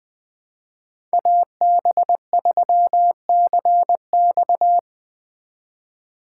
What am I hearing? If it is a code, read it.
Morse code, heard as a keyed tone, about 20 wpm, 700 Hz. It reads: AB3CX
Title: USA Call Signs - 5 Characters - 3x Character Spacing - 20wpm